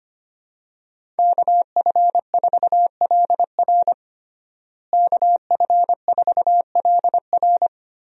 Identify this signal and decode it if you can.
Morse code, heard as a keyed tone, about 25 wpm, 700 Hz. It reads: KF4LR KF4LR
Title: USA Call Signs - 5 Characters - Repeated 3x - 25wpm